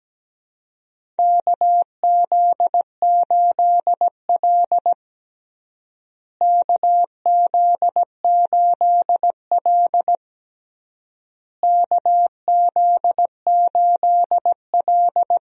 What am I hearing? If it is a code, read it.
Morse code, heard as a keyed tone, about 17 wpm, 700 Hz. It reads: KZ8L KZ8L KZ8L
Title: USA Call Signs - 4 Characters - Repeated 3x - 17wpm